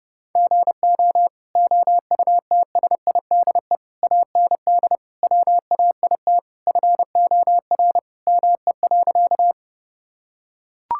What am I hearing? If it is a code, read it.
Morse code, heard as a keyed tone, about 32 wpm, 700 Hz. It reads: GO OUTSIDE AND WAIT FOR ME.
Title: Sentences from Top 1000 Words - Review Part 6 - 30wpm